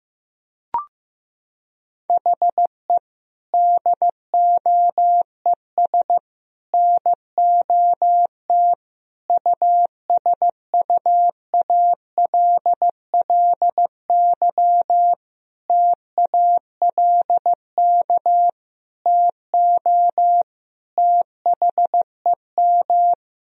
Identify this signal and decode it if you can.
Morse code, heard as a keyed tone, about 15 wpm, 700 Hz. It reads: HE DOES NOT USUALLY TALK TO THEM
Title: Sentences from Top 1000 Words - Review Part 3 - 15wpm